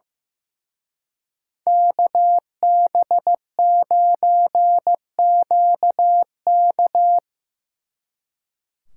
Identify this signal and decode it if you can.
Morse code, heard as a keyed tone, about 17 wpm, 700 Hz. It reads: KB9QK
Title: Speed Racing - USA Call Signs 15wpm